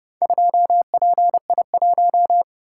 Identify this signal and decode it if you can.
Morse code, heard as a keyed tone, about 31 wpm, 700 Hz. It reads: 2PI1